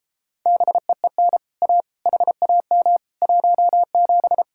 Morse code, heard as a keyed tone, about 33 wpm, 700 Hz. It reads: BEEN A HAM 17